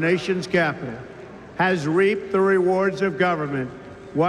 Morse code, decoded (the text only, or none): none